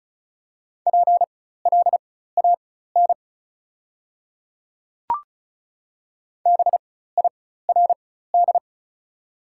Morse code, heard as a keyed tone, about 36 wpm, spaced as for 16 wpm, 700 Hz. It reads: PLAN BIRD
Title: 4x Character Spacing - Top 500 Words - 35wpm